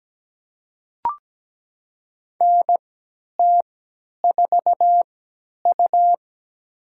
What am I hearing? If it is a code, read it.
Morse code, heard as a keyed tone, about 17 wpm, spaced as for 10 wpm, 700 Hz. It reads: NT4U
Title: USA Call Signs - 4 Characters - 3x Character Spacing - 17wpm